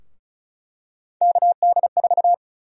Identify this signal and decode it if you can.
Morse code, heard as a keyed tone, about 34 wpm, 700 Hz. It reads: KD4
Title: USA Call Sign Prefixes - 35wpm